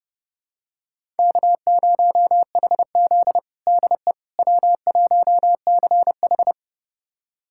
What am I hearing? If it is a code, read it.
Morse code, heard as a keyed tone, about 32 wpm, 700 Hz. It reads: K0HZ DE W1CH
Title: USA Callsign DE Callsign - 4 Characters - 30wpm